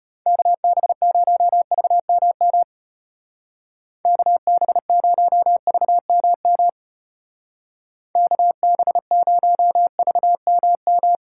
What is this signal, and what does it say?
Morse code, heard as a keyed tone, about 31 wpm, 700 Hz. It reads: KB0VMM KB0VMM KB0VMM
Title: Speed Racing - USA Call Signs 25wpm